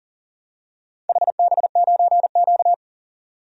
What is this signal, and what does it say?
Morse code, heard as a keyed tone, about 40 wpm, 700 Hz. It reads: HB9Q
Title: DXCC-100 Call Signs - 4 Characters - 40wpm